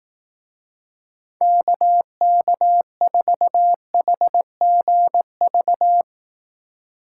Morse code, heard as a keyed tone, about 18 wpm, 700 Hz. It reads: KK4HGV
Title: Speed Racing - USA 2x3 Call Signs 15wpm